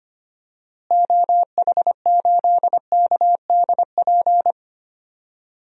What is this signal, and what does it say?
Morse code, heard as a keyed tone, about 25 wpm, 700 Hz. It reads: OH8KDP
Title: DXCC-100 Call Signs - 6 Characters - Repeated 2x - 25wpm